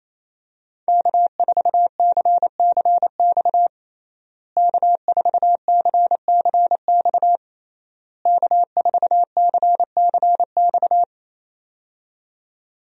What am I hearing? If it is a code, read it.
Morse code, heard as a keyed tone, about 28 wpm, 700 Hz. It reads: K4CCX K4CCX K4CCX
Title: USA Call Signs - 5 Characters - Repeated 3x - 28wpm